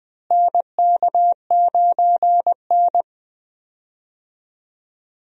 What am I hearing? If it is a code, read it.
Morse code, heard as a keyed tone, about 20 wpm, 700 Hz. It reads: NK9N
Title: USA Call Signs - 4 Characters - Repeated 3x - 20wpm